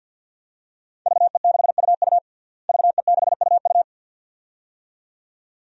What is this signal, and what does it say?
Morse code, heard as a keyed tone, about 53 wpm, 700 Hz. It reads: VE6UU VE6UU
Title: DXCC-100 Call Signs - 5 Characters - Repeated 2x - 50wpm